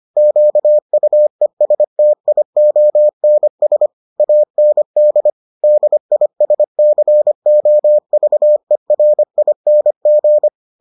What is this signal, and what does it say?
Morse code, heard as a keyed tone, about 25 wpm, 600 Hz. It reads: QUESTIONS AND DISCOVERING